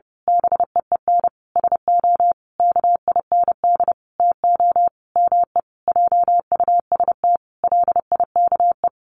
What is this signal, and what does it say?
Morse code, heard as a keyed tone, about 28 wpm, 700 Hz. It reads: BEEN SO KIND TO ME JUST LIKE
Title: Sentences from Top 1200 Words 30wpm